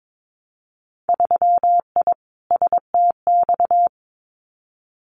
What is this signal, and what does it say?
Morse code, heard as a keyed tone, about 22 wpm, 700 Hz. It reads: 3I STX
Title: Winter Field Day Exchanges 22wpm